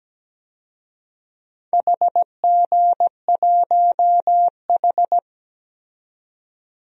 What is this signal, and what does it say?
Morse code, heard as a keyed tone, about 17 wpm, 700 Hz. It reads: HG1H
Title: DXCC-100 Call Signs - 4 Characters - 17wpm